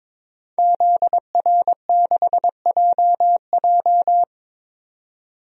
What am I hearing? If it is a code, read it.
Morse code, heard as a keyed tone, about 22 wpm, 700 Hz. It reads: ZR6JJ
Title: DXCC-100 Call Signs - 5 Characters - 22wpm